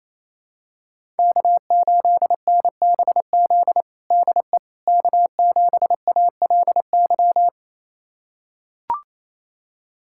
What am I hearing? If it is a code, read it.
Morse code, heard as a keyed tone, about 28 wpm, 700 Hz. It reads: K8NBZ DE K7ALY
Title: USA Callsign DE Callsign - 5 Characters - 28wpm